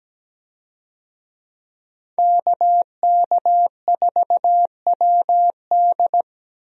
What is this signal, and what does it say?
Morse code, heard as a keyed tone, about 17 wpm, 700 Hz. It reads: KK4WD
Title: USA Call Signs - 5 Characters - 17wpm